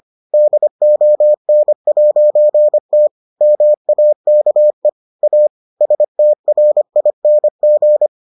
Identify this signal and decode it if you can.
Morse code, heard as a keyed tone, about 25 wpm, 600 Hz. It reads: DON'T MAKE A STRING